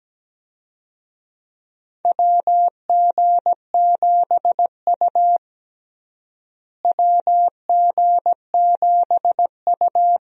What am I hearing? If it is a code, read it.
Morse code, heard as a keyed tone, about 17 wpm, 700 Hz. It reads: WG7U WG7U
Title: USA Call Signs - 4 Characters - Repeated 3x - 17wpm